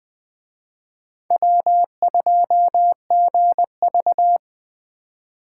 Morse code, heard as a keyed tone, about 21 wpm, 700 Hz. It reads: W2GV